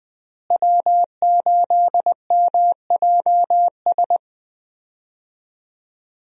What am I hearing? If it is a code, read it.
Morse code, heard as a keyed tone, about 20 wpm, 700 Hz. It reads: W8MJS
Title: USA Call Signs - 5 Characters - Repeated 2x - 20wpm